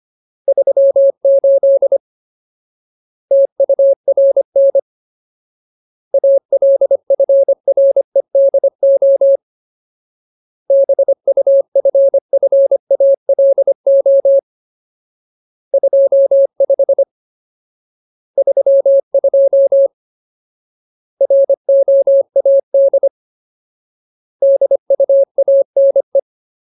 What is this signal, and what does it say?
Morse code, heard as a keyed tone, about 25 wpm, 550 Hz. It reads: 38 TURN ALFREDO BUFFALO 25 32 ROAD DUANE
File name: Ultimate Word List / 25WPM Ultimate